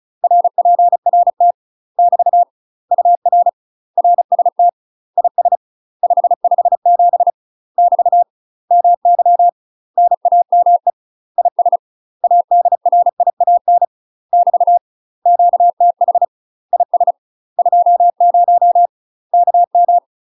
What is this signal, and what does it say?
Morse code, heard as a keyed tone, about 36 wpm, 700 Hz. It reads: RPRT <BT> UR RST IS 557 <BT> MY NAME IS ADRIAN <BT> QTH IS 20 KM
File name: UK MASTER QSOs TRAINING / 35WPM / UK MASTER QSOs 0002